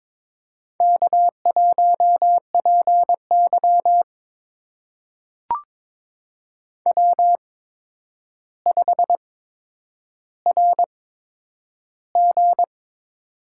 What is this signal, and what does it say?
Morse code, heard as a keyed tone, about 22 wpm, spaced as for 9 wpm, 700 Hz. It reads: K1PY W5RG
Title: USA Call Signs - 4 Characters - 8x Character Spacing - 22wpm